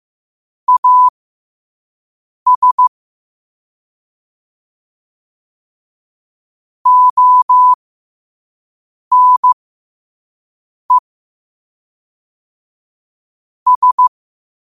Morse code, heard as a keyed tone, about 15 wpm, spaced as for 5 wpm, 1000 Hz. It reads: AS ONE S
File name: qotdfiles / 5wpm